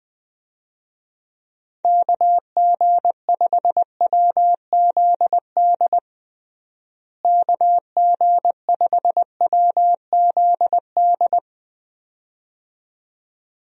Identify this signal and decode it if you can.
Morse code, heard as a keyed tone, about 20 wpm, 700 Hz. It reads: KG5WZD KG5WZD
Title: USA Call Signs - 6 Characters - Repeated 2x - 20wpm